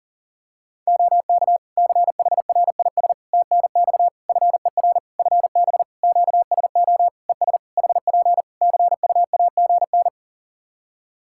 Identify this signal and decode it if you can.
Morse code, heard as a keyed tone, about 40 wpm, 700 Hz. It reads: OK CHRIS TNX FER FB QSO ES HP CUAGN
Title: Practice QSO - Exchange 4 - Ending - 40wpm